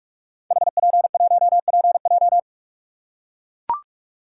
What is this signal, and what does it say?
Morse code, heard as a keyed tone, about 44 wpm, 700 Hz. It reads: HP1PJ
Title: Code Groups - 5 Letter:Numbers - 45wpm